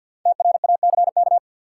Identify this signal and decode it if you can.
Morse code, heard as a keyed tone, about 52 wpm, 700 Hz. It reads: TRACK
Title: Five Letter Words - 50wpm